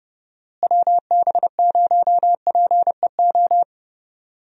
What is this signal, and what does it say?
Morse code, heard as a keyed tone, about 30 wpm, 700 Hz. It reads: WB0PEO
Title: USA Call Signs - 6 Characters - Repeated 2x - 30wpm